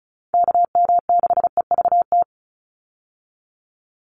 Morse code, heard as a keyed tone, about 34 wpm, 700 Hz. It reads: KM6EVT
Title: USA Call Signs 35wpm